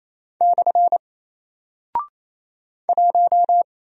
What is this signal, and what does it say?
Morse code, heard as a keyed tone, about 28 wpm, 700 Hz. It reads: / 1
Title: Single Character (Letter:Number:Punctuation:Prosign) Alt 28wpm